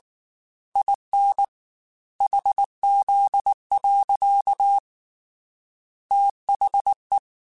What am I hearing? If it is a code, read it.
Morse code, heard as a keyed tone, about 19 wpm, 800 Hz. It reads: IN HZ. THE